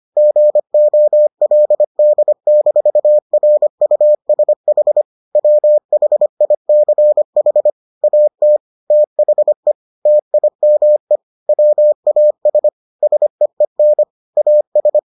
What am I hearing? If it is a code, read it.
Morse code, heard as a keyed tone, about 25 wpm, 600 Hz. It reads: GOLD-RUSH WHICH AT THE TIME WAS SEEN AS